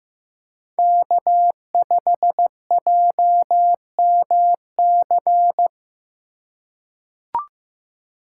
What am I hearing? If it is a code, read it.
Morse code, heard as a keyed tone, about 15 wpm, 700 Hz. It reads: K5JMC
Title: USA Call Signs - 5 Characters - 2x Character Spacing - 15wpm